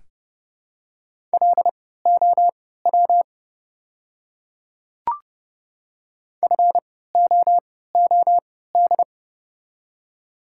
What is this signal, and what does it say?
Morse code, heard as a keyed tone, about 31 wpm, spaced as for 17 wpm, 700 Hz. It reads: LOW FOOD
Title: 3x Character Spacing - Top 300 Words - 30wpm